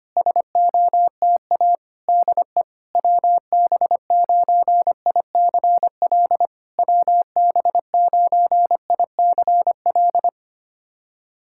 Morse code, heard as a keyed tone, about 25 wpm, 700 Hz. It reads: SOTA DE WB9ICL WB9ICL
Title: SOTA QSO Practice 25wpm